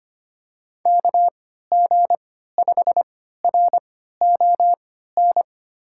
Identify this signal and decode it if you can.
Morse code, heard as a keyed tone, about 25 wpm, spaced as for 14 wpm, 700 Hz. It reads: KG5RON
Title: USA Call Signs - 6 Characters - 3x Character Spacing - 25wpm